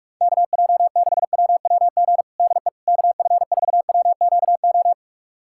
Morse code, heard as a keyed tone, about 46 wpm, 700 Hz. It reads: KJ6WWG DE KF4WQO
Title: USA Callsign DE Callsign - 6 Characters - 45wpm